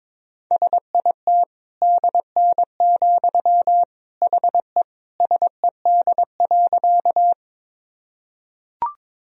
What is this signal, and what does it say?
Morse code, heard as a keyed tone, about 22 wpm, 700 Hz. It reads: SIT DN, HE SED.
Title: Sentences from Top 500 Words and Abbreviations 22wpm